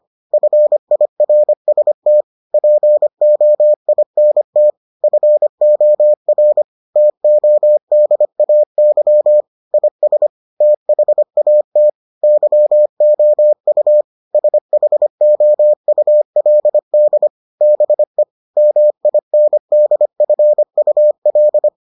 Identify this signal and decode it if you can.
Morse code, heard as a keyed tone, about 25 wpm, 600 Hz. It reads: FIRST POINT FOR TODAY IS THAT YOU SHOULD BE MINDFUL